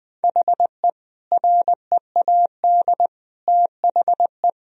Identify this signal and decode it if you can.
Morse code, heard as a keyed tone, about 20 wpm, 700 Hz. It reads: HE READ THE
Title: Sentences from Top 1000 Words - Review Part 4 - 20wpm